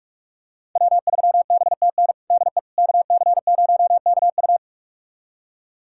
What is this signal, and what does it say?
Morse code, heard as a keyed tone, about 41 wpm, 700 Hz. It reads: W3BTN DE KC0KU
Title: USA Callsign DE Callsign - 5 Characters - 45wpm